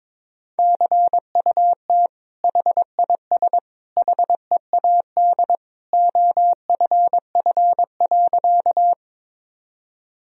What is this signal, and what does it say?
Morse code, heard as a keyed tone, about 22 wpm, 700 Hz. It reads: CUT HIS HEAD OFF.
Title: Sentences from Top 1000 Words - Review Part 1 - 22wpm